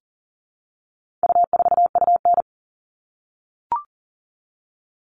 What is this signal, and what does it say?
Morse code, heard as a keyed tone, about 37 wpm, 700 Hz. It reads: U4UN
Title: All Cut Numbers - 4-Digit 40wpm